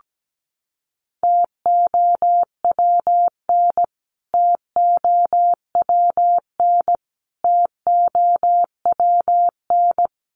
Morse code, heard as a keyed tone, about 17 wpm, 700 Hz. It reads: TOWN TOWN TOWN
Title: Top 500 Words Repeated 17wpm